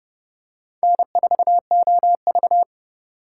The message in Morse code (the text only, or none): N4OV